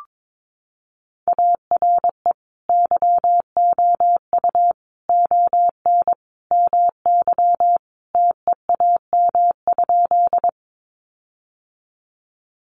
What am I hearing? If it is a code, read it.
Morse code, heard as a keyed tone, about 22 wpm, 700 Hz. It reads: ARE YOU ON MY TEAM?
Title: Sentences from Top 1500 Words 22wpm